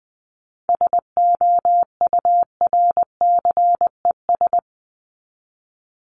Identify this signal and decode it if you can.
Morse code, heard as a keyed tone, about 20 wpm, 700 Hz. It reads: SOURCES
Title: ES:S Suffix - Top 100 Words with Base Words 20wpm